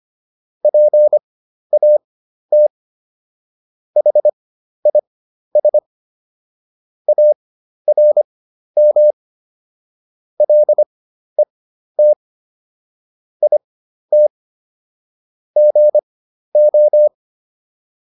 Morse code, heard as a keyed tone, about 25 wpm, spaced as for 12 wpm, 600 Hz. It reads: PAT HIS ARM LET IT GO